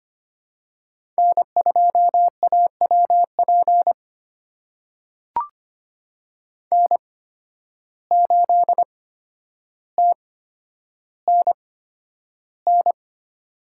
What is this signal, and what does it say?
Morse code, heard as a keyed tone, about 25 wpm, spaced as for 11 wpm, 700 Hz. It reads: N2AWP N8TNN
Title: USA Call Signs - 5 Characters - 8x Character Spacing - 25wpm